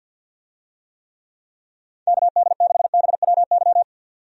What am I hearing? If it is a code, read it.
Morse code, heard as a keyed tone, about 51 wpm, 700 Hz. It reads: KD6BWY
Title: USA Call Signs - 6 Characters - 50wpm